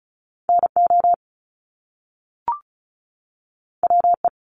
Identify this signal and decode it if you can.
Morse code, heard as a keyed tone, about 36 wpm, 700 Hz. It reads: NO WE
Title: Top Two Letter Combinations in Words 35wpm